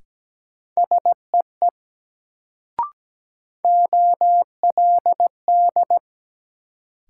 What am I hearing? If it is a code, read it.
Morse code, heard as a keyed tone, about 17 wpm, 700 Hz. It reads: SEE OLD
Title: Sets of 2 Words - New Word Drill - 17wpm